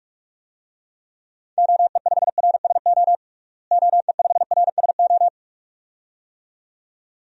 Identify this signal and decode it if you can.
Morse code, heard as a keyed tone, about 42 wpm, 700 Hz. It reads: OE5RSO OE5RSO
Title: DXCC-100 Call Signs - 6 Characters - Repeated 2x - 45wpm